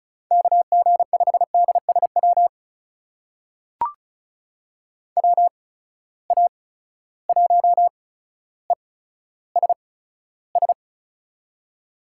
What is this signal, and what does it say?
Morse code, heard as a keyed tone, about 34 wpm, spaced as for 15 wpm, 700 Hz. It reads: KG5DSW WA1ESS
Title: USA Call Signs - 6 Characters - 8x Character Spacing - 35wpm